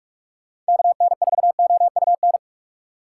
Morse code, heard as a keyed tone, about 68 wpm, spaced as for 56 wpm, 700 Hz. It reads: KN4OUN
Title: Speed Racing - USA 2x3 Call Signs 45wpm